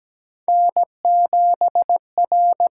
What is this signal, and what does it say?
Morse code, heard as a keyed tone, about 17 wpm, 700 Hz. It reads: N7R
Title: USA Call Signs - 5 Characters - 8x Character Spacing - 17wpm